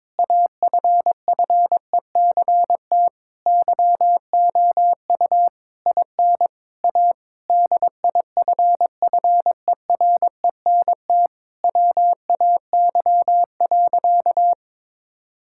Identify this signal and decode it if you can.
Morse code, heard as a keyed tone, about 22 wpm, 700 Hz. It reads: AFFECT YOU IN A DIFFERENT WAY.
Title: Sentences from Top 1000 Words - Review Part 6 - 22wpm